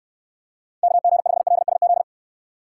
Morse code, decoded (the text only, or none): KG5LSZ